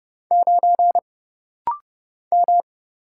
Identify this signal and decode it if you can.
Morse code, heard as a keyed tone, about 30 wpm, 700 Hz. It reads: 9 M
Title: Single Letter-Number Alt - Rapid-Fire 30wpm